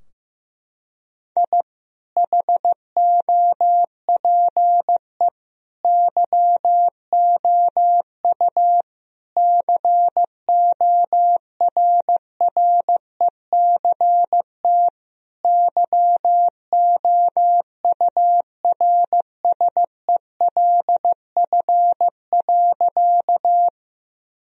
Morse code, heard as a keyed tone, about 15 wpm, 700 Hz. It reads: I HOPE YOU CORRECT YOURSELF.
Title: Sentences from Top 1000 Words - Review Part 2 - 15wpm